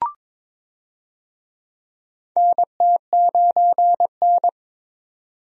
Morse code, heard as a keyed tone, about 22 wpm, 700 Hz. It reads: NT9N
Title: USA Call Signs - 4 Characters - 22wpm